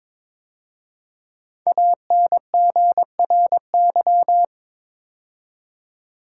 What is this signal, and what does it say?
Morse code, heard as a keyed tone, about 22 wpm, 700 Hz. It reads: ANGRY
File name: 5 Letter Words - First 3 Letters Given - 22wpm